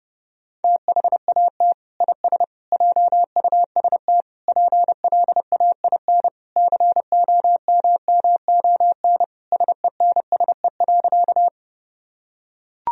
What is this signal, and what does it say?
Morse code, heard as a keyed tone, about 30 wpm, 700 Hz. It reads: THAT IS JUST PLAIN COMMON SENSE.
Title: Sentences from Top 1000 Words - Review Part 4 - 30wpm